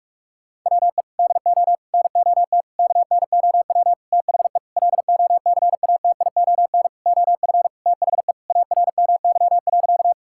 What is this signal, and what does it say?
Morse code, heard as a keyed tone, about 46 wpm, 690 Hz. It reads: WE DO NOT KNOW THE LOCATION OF THE ARMY.
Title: Sentences from Top 1000 Words - Review Part 6 - 45wpm